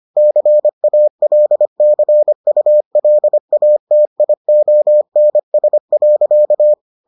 Morse code, heard as a keyed tone, about 25 wpm, 600 Hz. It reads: CALCULATIONS.